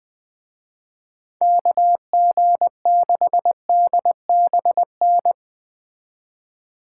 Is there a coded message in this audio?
Yes: Morse code from 1.4 to 5.3 s, heard as a keyed tone, about 20 wpm, 700 Hz. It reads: KG6DBN